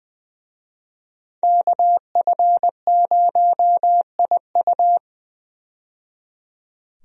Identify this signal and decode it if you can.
Morse code, heard as a keyed tone, about 20 wpm, 700 Hz. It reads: KF0IU
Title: USA Call Signs - 5 Characters - 20wpm